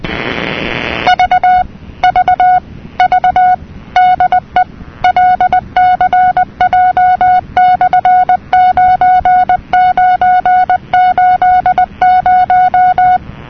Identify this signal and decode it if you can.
Morse code, heard as a keyed tone, about 20 wpm, 730 Hz. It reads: V V V DE LCJ/9980
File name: europe / nor / nor-237-ptt jeloey'lcj,...'-cw-(1985)-(rb)-(mc-909-2-10)